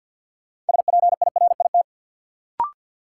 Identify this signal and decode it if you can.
Morse code, heard as a keyed tone, about 52 wpm, 700 Hz. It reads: SPIRIT
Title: Sentences from Top 1400 Words - New Word Drill Alt - 50wpm